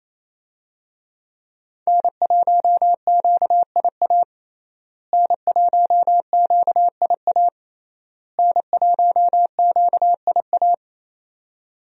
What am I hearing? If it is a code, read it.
Morse code, heard as a keyed tone, about 28 wpm, 700 Hz. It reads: N1QIA N1QIA N1QIA
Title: USA Call Signs - 5 Characters - Repeated 3x - 28wpm